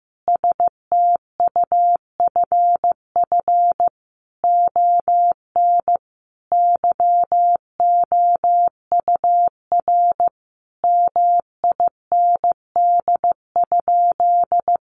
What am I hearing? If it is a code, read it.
Morse code, heard as a keyed tone, about 15 wpm, 700 Hz. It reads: STUFF ON YOUR MIND?
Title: Sentences from Top 1400 Words 15wpm